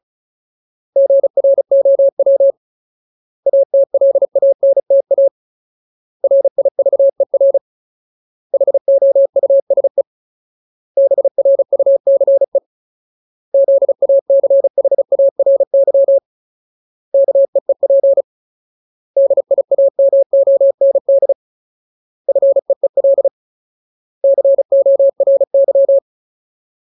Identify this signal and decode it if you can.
Morse code, heard as a keyed tone, about 34 wpm, 550 Hz. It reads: GROW ATLANTA RIVER HOUSE BRUCE ZACHARY KEEP DIAMOND FEEL CORY
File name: Ultimate Word List / 35WPM Ultimate